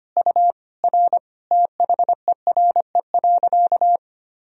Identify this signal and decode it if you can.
Morse code, heard as a keyed tone, about 25 wpm, 700 Hz. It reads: U R THERE.
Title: Sentences from Top 500 Words and Abbreviations 25wpm